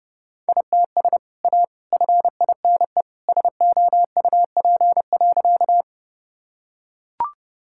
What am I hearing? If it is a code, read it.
Morse code, heard as a keyed tone, about 29 wpm, 700 Hz. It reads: ITS A FINE SOUP.
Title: Sentences from Top 1000 Words - Review Part 6 - 30wpm